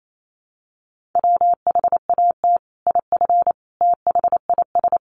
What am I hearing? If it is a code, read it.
Morse code, heard as a keyed tone, about 28 wpm, 700 Hz. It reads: WHAT IF THIS
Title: Sentences from Top 1500 Words 28wpm